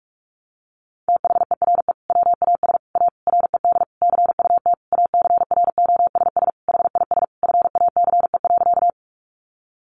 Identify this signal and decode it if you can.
Morse code, heard as a keyed tone, about 44 wpm, 700 Hz. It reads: THERE WAS A RED CUT ACROSS HIS FACE.
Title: Sentences from Top 700 Words 45wpm